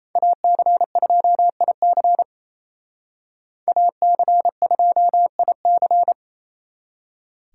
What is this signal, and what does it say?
Morse code, heard as a keyed tone, about 35 wpm, 700 Hz. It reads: AC2IC AC2IC